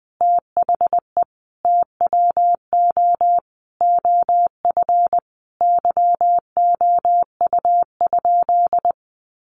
Morse code, heard as a keyed tone, about 20 wpm, 700 Hz. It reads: THE TWO OF YOU?
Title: Sentences from Top 1400 Words 20wpm